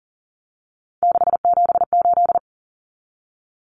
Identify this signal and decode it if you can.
Morse code, heard as a keyed tone, about 40 wpm, 700 Hz. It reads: 678
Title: Common Cut Numbers - 3-Digit 40wpm